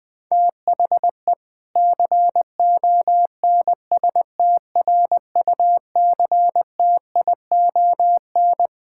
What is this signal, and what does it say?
Morse code, heard as a keyed tone, about 20 wpm, 700 Hz. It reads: THE CONSTRUCTION
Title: Sentences from Top 1600 Words 20wpm